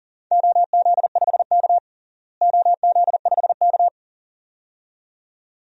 Morse code, heard as a keyed tone, about 40 wpm, 700 Hz. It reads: OZ5K OZ5K
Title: DXCC-100 Call Signs - 4 Characters - Repeated 2x - 40wpm